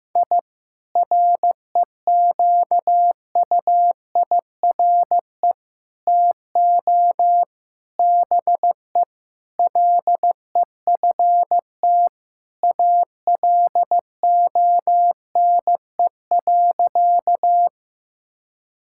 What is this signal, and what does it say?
Morse code, heard as a keyed tone, about 15 wpm, 700 Hz. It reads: I REQUIRE TO BE LEFT ALONE.
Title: Sentences from Top 1000 Words - Review Part 6 - 15wpm